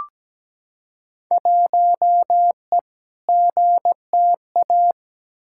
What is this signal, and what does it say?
Morse code, heard as a keyed tone, about 17 wpm, 700 Hz. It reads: 1E GTA
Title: Field Day Exchanges - 17wpm